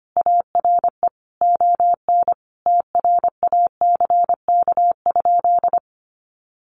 Morse code, heard as a keyed tone, about 25 wpm, 700 Hz. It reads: ARE ON TRACK?
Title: Sentences from Top 1200 Words 25wpm